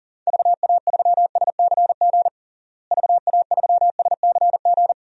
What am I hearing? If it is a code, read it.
Morse code, heard as a keyed tone, about 40 wpm, 700 Hz. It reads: VA3SCG VA3SCG
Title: DXCC-100 Call Signs - 6 Characters - Repeated 2x - 40wpm